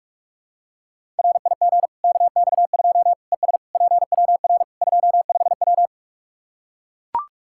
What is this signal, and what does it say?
Morse code, heard as a keyed tone, about 44 wpm, 700 Hz. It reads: RIG KX2 ES PWR 25W
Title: Practice QSO - Exchange 2 - Info - 45wpm